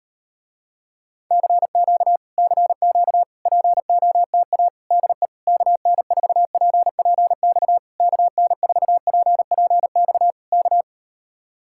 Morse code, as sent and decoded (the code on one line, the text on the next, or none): -.-. --.- / -.-. --.- / .--. --- - .- / -.. / -.- -. ....- .--. .--. -..- / -.- -. ....- .--. .--. -..- / -.-
CQ CQ POTA DE KN4PPX KN4PPX K